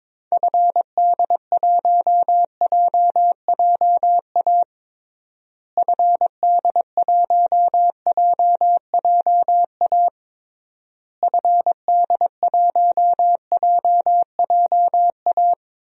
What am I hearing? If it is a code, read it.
Morse code, heard as a keyed tone, about 22 wpm, 700 Hz. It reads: FD1JJA FD1JJA FD1JJA
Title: DXCC-100 Call Signs - 6 Characters - Repeated 3x - 22wpm